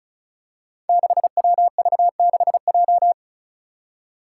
Morse code, heard as a keyed tone, about 36 wpm, 700 Hz. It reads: 6WV6J
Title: Five Challenging Letter-Number Combinations 35wpm